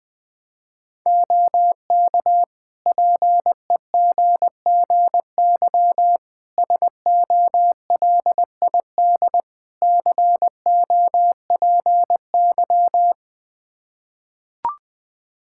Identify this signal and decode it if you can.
Morse code, heard as a keyed tone, about 20 wpm, 700 Hz. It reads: OK PEGGY SOLID COPY
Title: Practice QSO - Exchange 3 - More Info - 20wpm